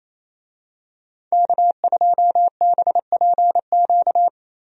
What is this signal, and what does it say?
Morse code, heard as a keyed tone, about 28 wpm, 700 Hz. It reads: K2BPQ
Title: USA Call Signs - 5 Characters - 28wpm